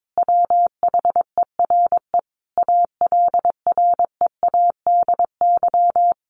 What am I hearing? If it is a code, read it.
Morse code, heard as a keyed tone, about 22 wpm, 700 Hz. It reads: WHERE ALREADY
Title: A Sound Of Thunder 22wpm